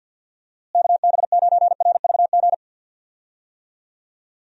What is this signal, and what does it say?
Morse code, heard as a keyed tone, about 52 wpm, 700 Hz. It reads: KB9RVG
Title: USA Call Signs - 6 Characters - Repeated 2x - 50wpm